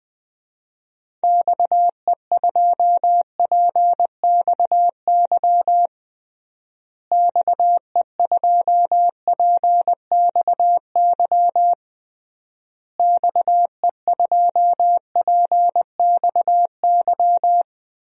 Morse code, heard as a keyed tone, about 21 wpm, 700 Hz. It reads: XE2PXY XE2PXY XE2PXY
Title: DXCC-100 Call Signs - 6 Characters - Repeated 3x - 20wpm